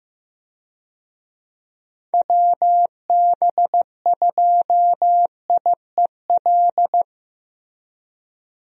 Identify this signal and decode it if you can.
Morse code, heard as a keyed tone, about 15 wpm, 700 Hz. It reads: WB2IEL
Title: USA Call Signs - 6 Characters - Repeated 2x - 15wpm